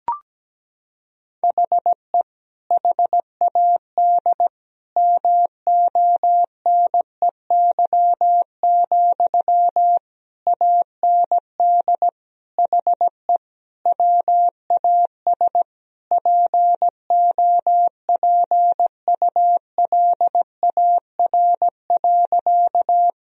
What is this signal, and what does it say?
Morse code, heard as a keyed tone, about 17 wpm, 700 Hz. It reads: HE HAD MONEY, AND HE WAS POPULAR.
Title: Sentences from Top 1000 Words - Review Part 3 - 17wpm